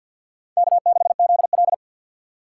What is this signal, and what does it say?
Morse code, heard as a keyed tone, about 62 wpm, 700 Hz. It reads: K6ZL
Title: Speed Racing - USA 1x2 Call Signs 50wpm